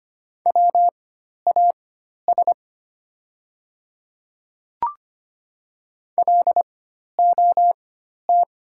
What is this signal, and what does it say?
Morse code, heard as a keyed tone, about 25 wpm, spaced as for 12 wpm, 700 Hz. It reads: WAS LOT
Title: 4x Character Spacing - Top 500 Words - 25wpm